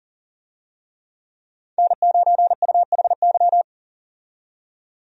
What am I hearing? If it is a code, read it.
Morse code, heard as a keyed tone, about 38 wpm, 700 Hz. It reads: N9UHY
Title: USA Call Signs - 5 Characters - 40wpm